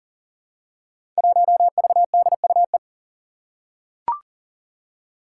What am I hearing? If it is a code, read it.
Morse code, heard as a keyed tone, about 39 wpm, 700 Hz. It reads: 1VDUE
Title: Five Challenging Letter-Number Combinations 40wpm